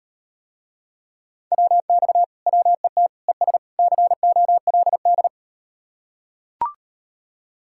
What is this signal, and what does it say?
Morse code, heard as a keyed tone, about 38 wpm, 700 Hz. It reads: WX WET ES COLD